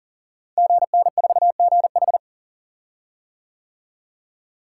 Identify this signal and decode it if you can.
Morse code, heard as a keyed tone, about 40 wpm, 700 Hz. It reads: GN4GH
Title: Code Groups - 5 Letter:Numbers - 40wpm